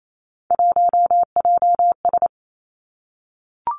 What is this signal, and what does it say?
Morse code, heard as a keyed tone, about 28 wpm, 700 Hz. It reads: 1JS
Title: Three Challenging Letter-Number Combinations 28wpm